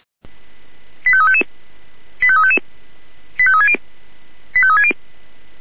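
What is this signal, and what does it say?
Five-tone selective call (ZVEI), heard as quick runs of five tones, tones of about 70 ms. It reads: 85355 85355 85360 85360
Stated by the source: ZVEI five-tone selective call received with an SDR.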